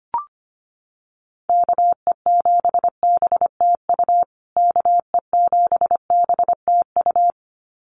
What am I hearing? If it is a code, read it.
Morse code, heard as a keyed tone, about 25 wpm, 700 Hz. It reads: KE7BTU KE7BTU
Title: USA Call Signs Repeated 2x 25wpm